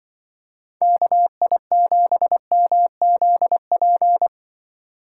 Morse code, heard as a keyed tone, about 23 wpm, 700 Hz. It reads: KI7MZP
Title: Speed Racing - USA 2x3 Call Signs 20wpm